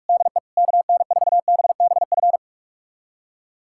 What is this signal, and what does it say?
Morse code, heard as a keyed tone, about 41 wpm, 700 Hz. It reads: DE KN4BBF
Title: USA Callsign DE Callsign - 6 Characters - 45wpm